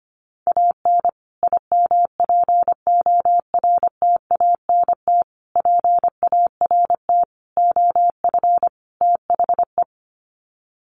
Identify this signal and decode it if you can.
Morse code, heard as a keyed tone, about 25 wpm, 700 Hz. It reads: AN IMPORTANT PART OF THE
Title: Sets of 5 Words Encore 25wpm